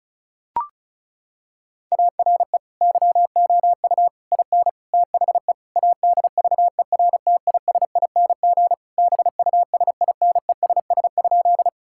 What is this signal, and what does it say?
Morse code, heard as a keyed tone, about 36 wpm, 700 Hz. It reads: ARE YOU IN THE ADVERTISING BUSINESS?
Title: Sentences from Top 1000 Words - Review Part 6 - 35wpm